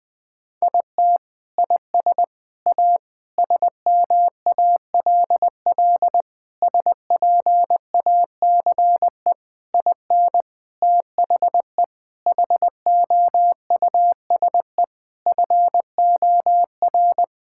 Morse code, heard as a keyed tone, about 20 wpm, 700 Hz. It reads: IT IS A SMALL SPACE IN THE HOUSE FOR
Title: Sentences from Top 1000 Words - Review Part 5 - 20wpm